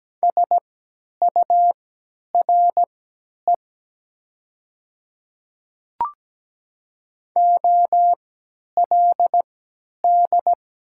Morse code, heard as a keyed tone, about 17 wpm, spaced as for 10 wpm, 700 Hz. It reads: SURE OLD
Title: 3x Character Spacing - Top 300 Words - 17wpm